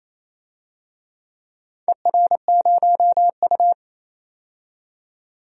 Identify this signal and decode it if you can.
Morse code, heard as a keyed tone, about 28 wpm, 690 Hz. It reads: ER0U
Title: DXCC-100 Call Signs - 4 Characters - 28wpm